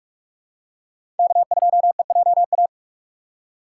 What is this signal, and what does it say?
Morse code, heard as a keyed tone, about 44 wpm, 700 Hz. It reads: K2EJA
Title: USA Call Signs - 5 Characters - 6x Character Spacing - 45wpm